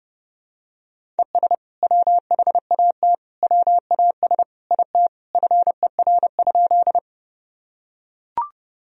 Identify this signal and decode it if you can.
Morse code, heard as a keyed tone, about 32 wpm, 700 Hz. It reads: ES WHAT WAS IT FER?
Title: Sentences from Top 100 Words and Abbreviations 30wpm